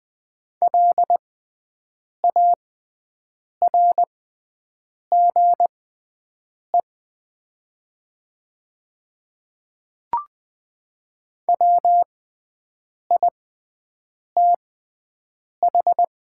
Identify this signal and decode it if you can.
Morse code, heard as a keyed tone, about 20 wpm, spaced as for 7 wpm, 700 Hz. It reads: LARGE WITH
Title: 6x Character Spacing - Top 300 Words - 20wpm